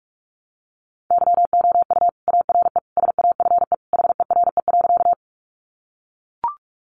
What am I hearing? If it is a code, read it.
Morse code, heard as a keyed tone, about 44 wpm, 700 Hz. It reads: YOU ARE SAFE HERE.
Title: Sentences from Top 900 Words 45wpm